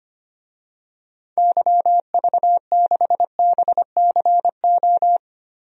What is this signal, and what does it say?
Morse code, heard as a keyed tone, about 25 wpm, 700 Hz. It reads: YV6BCO
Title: DXCC-100 Call Signs - 6 Characters - 25wpm